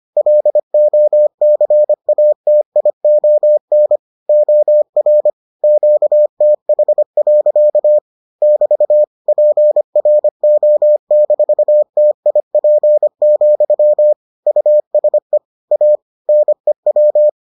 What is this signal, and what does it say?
Morse code, heard as a keyed tone, about 25 wpm, 600 Hz. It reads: LOCATION OR QTH. <BT> PRO-TIP, USE A NEW